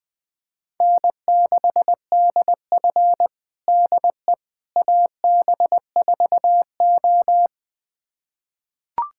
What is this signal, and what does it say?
Morse code, heard as a keyed tone, about 21 wpm, 700 Hz. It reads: N6DF DE AB4O
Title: USA Callsign DE Callsign - 4 Characters - 20wpm